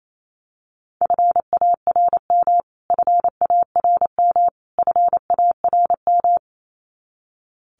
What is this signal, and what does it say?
Morse code, heard as a keyed tone, about 28 wpm, 700 Hz. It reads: FARM FARM FARM
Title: Top 500 Words Repeated 28wpm